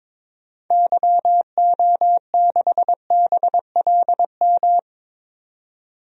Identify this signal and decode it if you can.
Morse code, heard as a keyed tone, about 22 wpm, 700 Hz. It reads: YO6BLM
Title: DXCC-100 Call Signs - 6 Characters - 22wpm